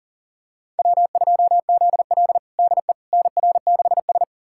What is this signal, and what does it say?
Morse code, heard as a keyed tone, about 39 wpm, 700 Hz. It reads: W2ZL DE NR6S
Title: USA Callsign DE Callsign - 4 Characters - 40wpm